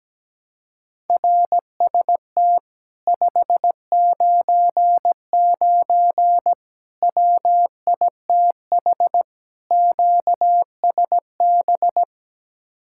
Morse code, heard as a keyed tone, about 17 wpm, 700 Hz. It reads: RST 599 WITH QSB
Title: Speed-Racing Common QSO Phrases - 17wpm